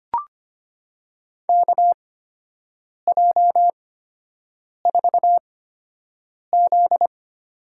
Morse code, heard as a keyed tone, about 25 wpm, spaced as for 7 wpm, 700 Hz. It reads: KJ4Z
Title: USA Call Signs - 6 Characters - 8x Character Spacing - 25wpm